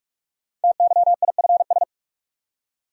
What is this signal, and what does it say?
Morse code, heard as a keyed tone, about 44 wpm, 700 Hz. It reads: TYIFS
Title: Code Groups - 5 Letter:Numbers - 45wpm